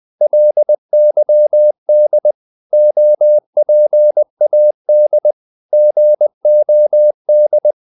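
Morse code, heard as a keyed tone, about 20 wpm, 600 Hz. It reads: LYD OPAD GOD
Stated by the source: Morse code practice of short Danish words at 20 wpm, from CW med Gnister.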